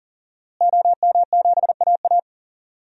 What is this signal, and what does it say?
Morse code, heard as a keyed tone, about 40 wpm, 700 Hz. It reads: OM7AA
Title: DXCC-100 Call Signs - 5 Characters - Repeated 3x - 40wpm